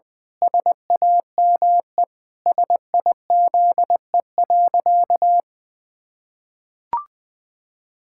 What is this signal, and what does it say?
Morse code, heard as a keyed tone, about 21 wpm, 700 Hz. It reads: SAME SIZE.
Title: Pangram Challenge v2 - 20wpm